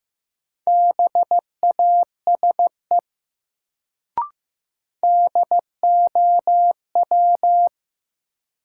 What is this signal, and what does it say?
Morse code, heard as a keyed tone, about 15 wpm, 700 Hz. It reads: BASE DOW
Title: Sets of 5 Words - New Word Drill - 15wpm